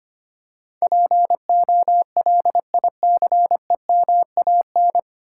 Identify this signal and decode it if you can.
Morse code, heard as a keyed tone, about 25 wpm, 700 Hz. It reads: POLICEMAN
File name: Nine Letter Words - 25wpm